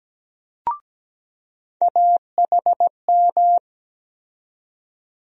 Morse code, heard as a keyed tone, about 17 wpm, 700 Hz. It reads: AHM
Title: USA Call Sign Postfixes - 3 Letters 17wpm